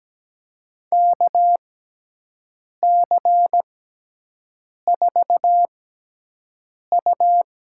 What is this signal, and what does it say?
Morse code, heard as a keyed tone, about 17 wpm, spaced as for 6 wpm, 700 Hz. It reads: KC4U
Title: USA Call Signs - 6 Characters - 6x Character Spacing - 17wpm